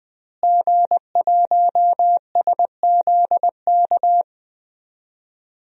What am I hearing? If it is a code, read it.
Morse code, heard as a keyed tone, about 20 wpm, 700 Hz. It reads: G1SZK
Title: DXCC-100 Call Signs - 5 Characters - Repeated 2x - 20wpm